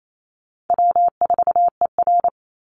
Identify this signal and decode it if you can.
Morse code, heard as a keyed tone, about 28 wpm, 700 Hz. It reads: W4ER
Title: Easy Call Signs - 28wpm